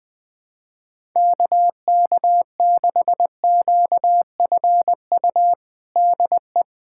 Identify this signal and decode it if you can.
Morse code, heard as a keyed tone, about 21 wpm, 700 Hz. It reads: KK6QFU DE
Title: USA Callsign DE Callsign - 6 Characters - 20wpm